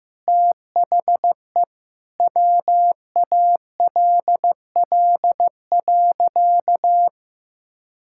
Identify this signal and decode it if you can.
Morse code, heard as a keyed tone, about 16 wpm, 700 Hz. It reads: THE WALL.
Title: Sentences from Top 1600 Words 15wpm